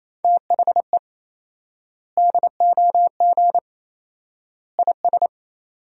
Morse code, heard as a keyed tone, about 27 wpm, 700 Hz. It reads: THE DOG IS
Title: 4x Word-Spacing - Sentences from Top 400 Words - 28wpm